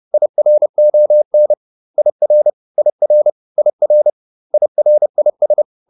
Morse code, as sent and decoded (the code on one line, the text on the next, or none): .. .-. --- -. / .. .-. / .. .-. / .. .-. / .. .-. .. ...
IRON IR IR IR IRIS